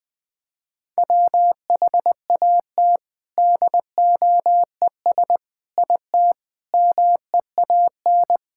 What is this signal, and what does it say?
Morse code, heard as a keyed tone, about 20 wpm, 700 Hz. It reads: WHAT DOES IT MEAN
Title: Sentences from Top 1000 Words - Review Part 6 - 20wpm